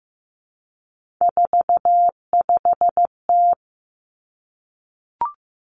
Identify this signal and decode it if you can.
Morse code, heard as a keyed tone, about 15 wpm, 700 Hz. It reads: 45T
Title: Three Challenging Letter-Number Combinations 15wpm